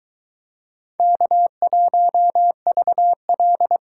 Morse code, heard as a keyed tone, about 22 wpm, 700 Hz. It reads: K1VL